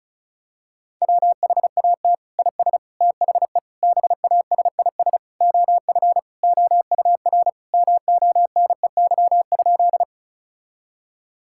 Morse code, heard as a keyed tone, about 34 wpm, 700 Hz. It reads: WHAT IS THE BASIS OF OUR MONEY?
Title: Sentences from Top 1000 Words - Review Part 5 - 35wpm